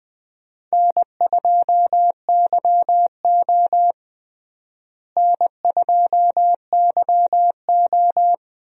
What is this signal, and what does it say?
Morse code, heard as a keyed tone, about 20 wpm, 700 Hz. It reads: N2YO N2YO
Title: USA Call Signs - 4 Characters - Repeated 3x - 20wpm